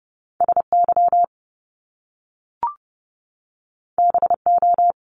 Morse code, heard as a keyed tone, about 30 wpm, 700 Hz. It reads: SY BO
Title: Top Two Letter Combinations in Words 30wpm